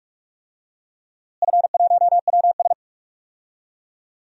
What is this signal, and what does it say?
Morse code, heard as a keyed tone, about 43 wpm, 700 Hz. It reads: F1WS
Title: DXCC-100 Call Signs - 4 Characters - Repeated 3x - 45wpm